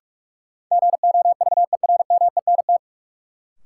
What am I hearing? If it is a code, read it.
Morse code, heard as a keyed tone, about 46 wpm, 700 Hz. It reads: GOVERMENT
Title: Sentences from Top 1300 Words - New Word Drill Alt - 45wpm